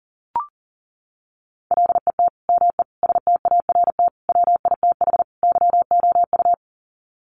Morse code, heard as a keyed tone, about 40 wpm, 700 Hz. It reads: LET ME START WITH YOU
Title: Sets of 5 Words Encore 40wpm